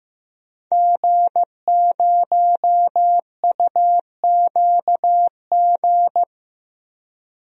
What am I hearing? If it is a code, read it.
Morse code, heard as a keyed tone, about 15 wpm, 700 Hz. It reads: G0UQG
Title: DXCC-100 Call Signs - 5 Characters - Repeated 2x - 15wpm